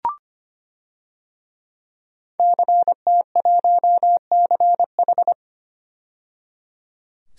Morse code, heard as a keyed tone, about 25 wpm, 700 Hz. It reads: CT1CH
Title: DXCC-100 Call Signs - 5 Characters - 25wpm